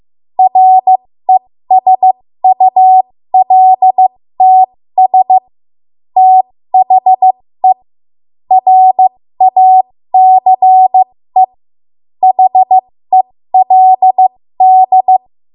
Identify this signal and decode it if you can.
Morse code, heard as a keyed tone, about 15 wpm, spaced as for 13 wpm, 750 Hz. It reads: RESULTS THE RACE HELD
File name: Morse / 13WPM